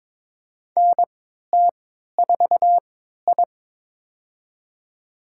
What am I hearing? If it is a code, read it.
Morse code, heard as a keyed tone, about 22 wpm, spaced as for 13 wpm, 700 Hz. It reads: NT4I
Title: USA Call Signs - 4 Characters - 3x Character Spacing - 22wpm